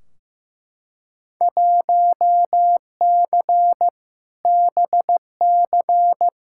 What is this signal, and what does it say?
Morse code, heard as a keyed tone, about 15 wpm, 700 Hz. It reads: 1C BC
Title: Field Day Exchanges Encore - 15wpm